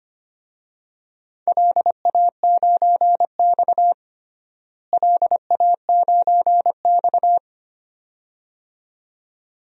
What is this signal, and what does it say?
Morse code, heard as a keyed tone, about 25 wpm, 700 Hz. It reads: LA9X LA9X
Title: DXCC-100 Call Signs - 4 Characters - Repeated 2x - 25wpm